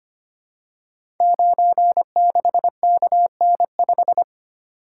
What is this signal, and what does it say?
Morse code, heard as a keyed tone, about 25 wpm, 700 Hz. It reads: 96KN5
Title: Code Groups - 5 Letter:Numbers - 25wpm